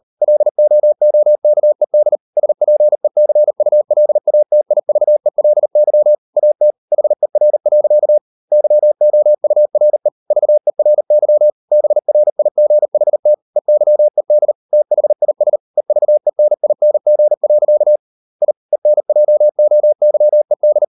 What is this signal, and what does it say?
Morse code, heard as a keyed tone, about 40 wpm, 600 Hz. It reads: LOOKED SPECULATIVELY AT HER. YOURE VERY BRIGHT EYED THIS EVENING. I ENJOYED